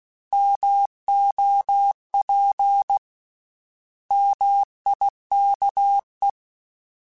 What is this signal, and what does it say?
Morse code, heard as a keyed tone, about 16 wpm, 800 Hz. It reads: MOP MIKE
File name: Misc / Misc-mp3s / Misc @15